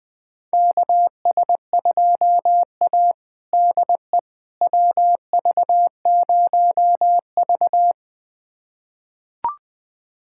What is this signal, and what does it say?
Morse code, heard as a keyed tone, about 20 wpm, 700 Hz. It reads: KS2A DE WV0V
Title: USA Callsign DE Callsign - 4 Characters - 20wpm